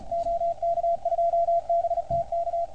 Morse code, heard as a keyed tone, about 28 wpm, 690 Hz, with radio noise. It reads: OK2BTK